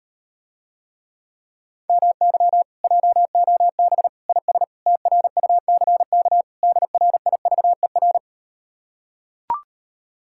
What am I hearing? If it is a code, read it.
Morse code, heard as a keyed tone, about 38 wpm, 700 Hz. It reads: MY JOB IS TRUCK DRIVER